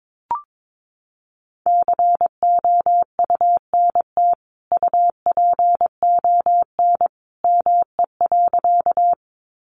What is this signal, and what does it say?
Morse code, heard as a keyed tone, about 22 wpm, 700 Hz. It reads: COUNT UPON ME.
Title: Sentences from Top 1500 Words 22wpm